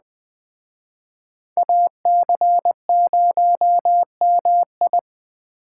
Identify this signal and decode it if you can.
Morse code, heard as a keyed tone, about 19 wpm, 700 Hz. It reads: AC0MI